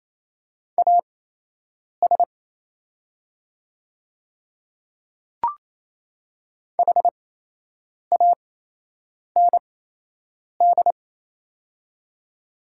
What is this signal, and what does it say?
Morse code, heard as a keyed tone, about 28 wpm, spaced as for 8 wpm, 700 Hz. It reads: AS HAND
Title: 8x Character Spacing - Top 200 Words - 28wpm